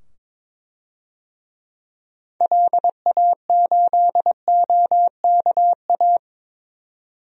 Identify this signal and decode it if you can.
Morse code, heard as a keyed tone, about 22 wpm, 700 Hz. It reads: LA8OKA
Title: DXCC-100 Call Signs - 6 Characters - 22wpm